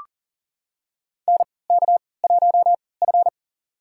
Morse code, heard as a keyed tone, about 40 wpm, spaced as for 23 wpm, 700 Hz. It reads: NK1F
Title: USA Call Signs - 3x Character Spacing - 40wpm